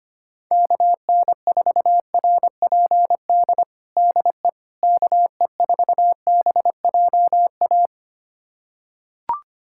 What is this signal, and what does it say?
Morse code, heard as a keyed tone, about 25 wpm, 700 Hz. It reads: KN4RPD DE KE4BJA
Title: USA Callsign DE Callsign - 6 Characters - 25wpm